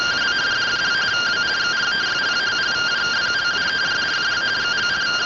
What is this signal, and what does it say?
Two-tone teleprinter signal, about 170 Hz shift.
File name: findings / CT SAGITAL 27.gen-11 rtty rx